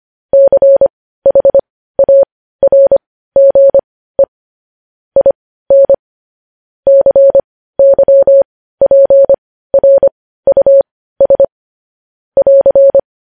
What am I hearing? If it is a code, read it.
Morse code, heard as a keyed tone, about 25 wpm, spaced as for 15 wpm, 550 Hz. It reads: CHARGE IN CYPRUS <AR>
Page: News Headlines in Morse Code at 15 WPM